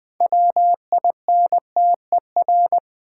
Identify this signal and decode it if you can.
Morse code, heard as a keyed tone, about 20 wpm, 700 Hz. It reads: WINTER
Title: Six Letter Words - 20wpm